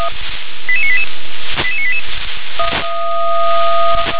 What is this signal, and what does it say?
Five-tone selective call (ZVEI), heard as quick runs of five tones, tones of about 70 ms. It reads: 88988 88988
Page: SDR/ZVEI